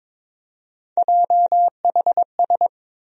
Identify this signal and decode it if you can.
Morse code, heard as a keyed tone, about 22 wpm, 700 Hz. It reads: JHS
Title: USA Call Sign Postfixes - 3 Letters 22wpm